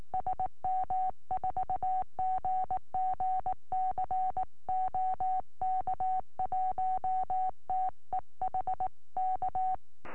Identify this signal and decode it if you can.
Morse code, heard as a keyed tone, about 19 wpm, 750 Hz. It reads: SM4GGC OK1TEH K